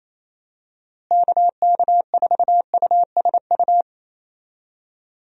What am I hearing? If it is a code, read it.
Morse code, heard as a keyed tone, about 28 wpm, 700 Hz. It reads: KK4USU
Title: USA Call Signs - 6 Characters - Repeated 3x - 28wpm